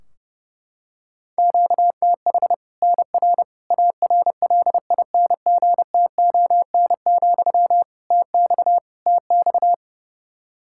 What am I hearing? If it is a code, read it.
Morse code, heard as a keyed tone, about 30 wpm, 700 Hz. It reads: QTH NR ARLINGTON, TX TX
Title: K1USN SST QSO Practice 30wpm